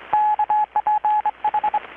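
Morse code, heard as a keyed tone, about 24 wpm, 850 Hz. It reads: KPH
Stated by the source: Morse from KPH's Night of Nights XX maritime radiotelegraphy event, received in Chiapas, Mexico about 3,800 km away on a Kenwood R-600 with a grounded 10 m long-wire antenna.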